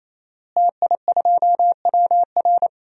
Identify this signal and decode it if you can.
Morse code, heard as a keyed tone, about 28 wpm, 700 Hz. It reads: TI2WR
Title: DXCC-100 Call Signs - 5 Characters - Repeated 3x - 28wpm